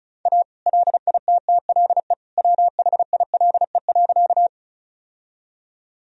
Morse code, heard as a keyed tone, about 36 wpm, 700 Hz. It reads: A LITTLE WHILE.
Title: Sentences from Top 1000 Words - Review Part 1 - 35wpm